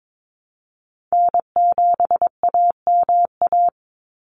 Morse code, heard as a keyed tone, about 22 wpm, 700 Hz. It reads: N7AMA
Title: NA SOTA Chaser Callsigns 22wpm